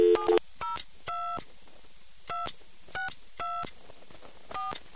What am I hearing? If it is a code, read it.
DTMF, heard as telephone touch tones, tones of about 170 ms. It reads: 7022624